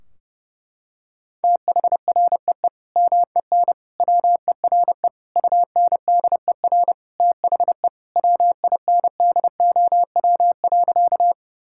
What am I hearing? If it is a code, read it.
Morse code, heard as a keyed tone, about 30 wpm, 700 Hz. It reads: THREE MEN WERE UNDER THE WINDOW.
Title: Sentences from Top 1600 Words 30wpm